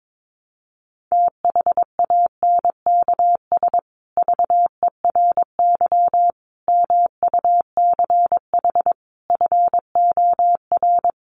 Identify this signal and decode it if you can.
Morse code, heard as a keyed tone, about 22 wpm, 700 Hz. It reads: THANKS VERY MUCH FOR
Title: Sets of 5 Words Encore2 22wpm